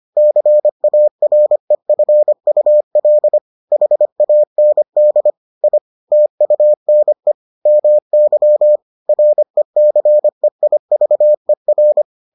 Morse code, heard as a keyed tone, about 25 wpm, 600 Hz. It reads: CAREFUL HAND I TUNE MY RECEIVER